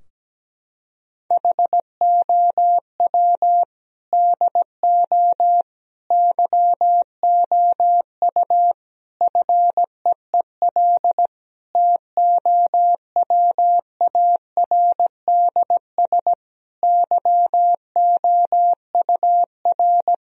Morse code, heard as a keyed tone, about 17 wpm, 700 Hz. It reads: HOW DO YOU FEEL TOWARDS YOUR
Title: Sentences from Top 1000 Words - Review Part 2 - 17wpm